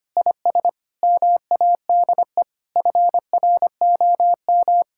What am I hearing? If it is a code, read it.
Morse code, heard as a keyed tone, about 25 wpm, 700 Hz. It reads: IS MADE FROM
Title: Sentences from Top 1000 Words - Review Part 6 - 25wpm